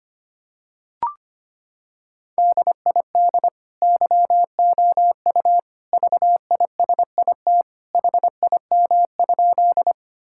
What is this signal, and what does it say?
Morse code, heard as a keyed tone, about 25 wpm, 700 Hz. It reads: DID YOU VISIT HIM?
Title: Sentences from Top 1000 Words - Review Part 6 - 25wpm